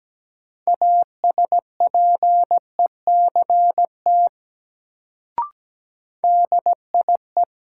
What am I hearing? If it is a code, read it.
Morse code, heard as a keyed tone, about 17 wpm, 700 Hz. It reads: ASPECT DIE
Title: Sentences from Top 1300 Words - New Word Drill Alt - 17wpm